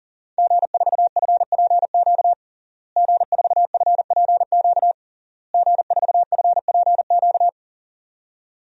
Morse code, heard as a keyed tone, about 40 wpm, 700 Hz. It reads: G4FPQ G4FPQ G4FPQ
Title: DXCC-100 Call Signs - 5 Characters - Repeated 3x - 40wpm